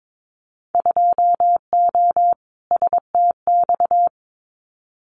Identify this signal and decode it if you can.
Morse code, heard as a keyed tone, about 22 wpm, 700 Hz. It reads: 2O STX
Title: Winter Field Day Exchanges 22wpm